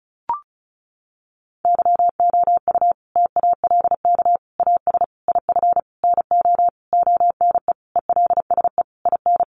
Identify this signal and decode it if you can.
Morse code, heard as a keyed tone, about 34 wpm, 700 Hz. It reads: YOU TALK AS IF NO ONE ELSE IN
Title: Sentences from Top 1100 Words 35wpm